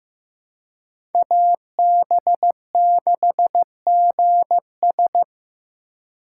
Morse code, heard as a keyed tone, about 16 wpm, 700 Hz. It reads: AB6GS
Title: USA Call Signs - 5 Characters - 2x Character Spacing - 15wpm